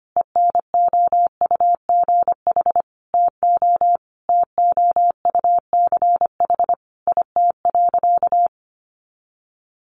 Morse code, heard as a keyed tone, about 25 wpm, 700 Hz. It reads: ENOUGH TO TOUCH IT.
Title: Sentences from Top 1200 Words 25wpm